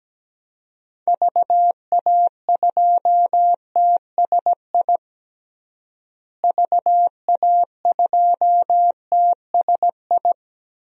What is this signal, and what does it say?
Morse code, heard as a keyed tone, about 17 wpm, 700 Hz. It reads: VA2TSI VA2TSI
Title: DXCC-100 Call Signs - 6 Characters - Repeated 2x - 17wpm